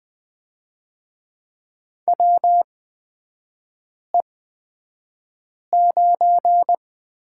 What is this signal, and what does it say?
Morse code, heard as a keyed tone, about 20 wpm, spaced as for 5 wpm, 690 Hz. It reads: WE9
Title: 229 Lesson 37e - Letter x - Three Characters 20wpm